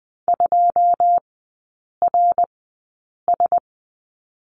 Morse code, heard as a keyed tone, about 20 wpm, spaced as for 8 wpm, 690 Hz. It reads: 2RS
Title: 191 Lesson 31e - Letter v - Three Characters 20wpm